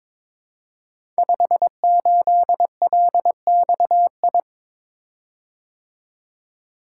Morse code, heard as a keyed tone, about 23 wpm, 700 Hz. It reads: 58LXI
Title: Code Groups - 5 Letter:Numbers - 22wpm